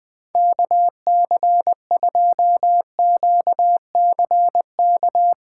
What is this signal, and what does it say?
Morse code, heard as a keyed tone, about 20 wpm, 700 Hz. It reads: KC2QCK
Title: USA Call Signs - 2x Character Spacing - 20wpm